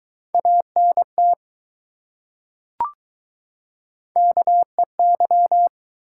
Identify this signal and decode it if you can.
Morse code, heard as a keyed tone, about 20 wpm, 700 Hz. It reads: ANT KEY